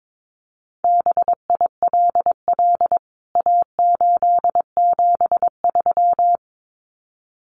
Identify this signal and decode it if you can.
Morse code, heard as a keyed tone, about 22 wpm, 700 Hz. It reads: BILL A873
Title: CWT Exchange 22wpm